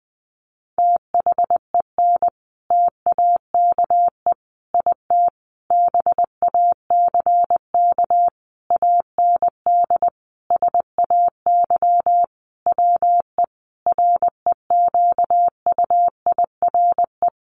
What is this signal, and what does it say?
Morse code, heard as a keyed tone, about 20 wpm, 700 Hz. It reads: THEN TAKE IT BACK AND SAY WE REQUIRE
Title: Sentences from Top 1500 Words 20wpm